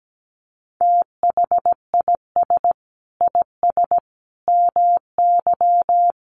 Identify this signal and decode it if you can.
Morse code, heard as a keyed tone, about 17 wpm, 700 Hz. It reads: THIS IS MY
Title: Sentences from Top 300 Words 17wpm